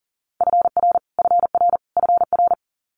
Morse code, heard as a keyed tone, about 40 wpm, 690 Hz. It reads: FR FR FR
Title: Multi-letter Phonemes Repeated 3x 40wpm